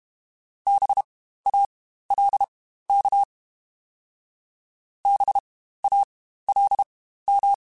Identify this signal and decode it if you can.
Morse code, heard as a keyed tone, about 32 wpm, spaced as for 15 wpm, 800 Hz. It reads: BALK BALM
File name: Spaced Alphas / Spaced Alphas mp3s / Bs sp @30